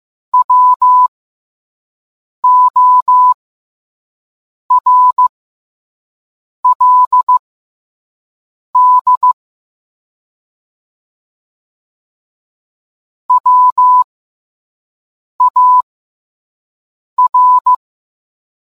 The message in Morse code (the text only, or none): WORLD WAR